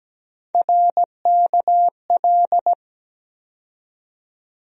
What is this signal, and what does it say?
Morse code, heard as a keyed tone, about 17 wpm, 700 Hz. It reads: RKL
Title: USA Call Sign Postfixes - 3 Letters 17wpm